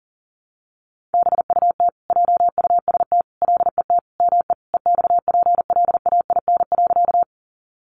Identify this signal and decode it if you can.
Morse code, heard as a keyed tone, about 40 wpm, 700 Hz. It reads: BUT JUST LET ME EXPLAIN.
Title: Sentences from Top 1100 Words 40wpm